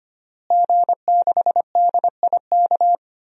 Morse code, heard as a keyed tone, about 25 wpm, 700 Hz. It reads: G6DIK
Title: DXCC-100 Call Signs - 5 Characters - Repeated 3x - 25wpm